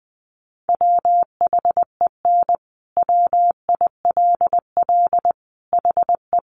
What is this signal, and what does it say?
Morse code, heard as a keyed tone, about 20 wpm, 700 Hz. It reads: WHEN WILL HE
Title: Sentences from Top 200 Words 20wpm